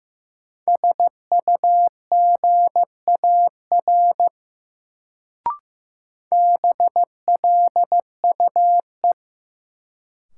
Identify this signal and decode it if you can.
Morse code, heard as a keyed tone, about 15 wpm, 700 Hz. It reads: SUGAR BLUE
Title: Sentences from Top 1200 Words - New Word Drill Alt - 15wpm